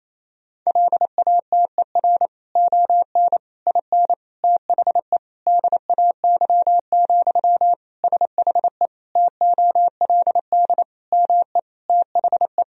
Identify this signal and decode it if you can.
Morse code, heard as a keyed tone, about 28 wpm, 700 Hz. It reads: LATER ON IN THE DAY, SHE TOLD ME THE
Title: Sentences from Top 1000 Words - Review Part 4 - 28wpm